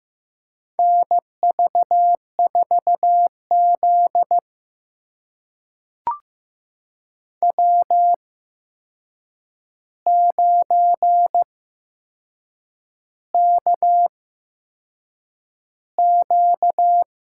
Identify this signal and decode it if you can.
Morse code, heard as a keyed tone, about 15 wpm, spaced as for 6 wpm, 700 Hz. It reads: NV4Z W9KQ
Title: USA Call Signs - 4 Characters - 8x Character Spacing - 15wpm